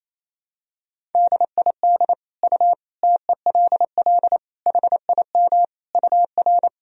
Morse code, heard as a keyed tone, about 28 wpm, 700 Hz. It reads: DID U TELL HIM UR
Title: Sentences from Top 400 Words and Abbreviations 28wpm